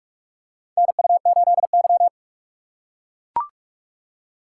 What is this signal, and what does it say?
Morse code, heard as a keyed tone, about 44 wpm, 700 Hz. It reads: NU8Y
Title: USA Call Signs - 4 Characters - 2x Character Spacing - 45wpm